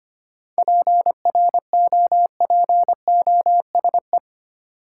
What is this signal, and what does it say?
Morse code, heard as a keyed tone, about 25 wpm, 700 Hz. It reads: PROPOSE
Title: Seven Letter Words - 25wpm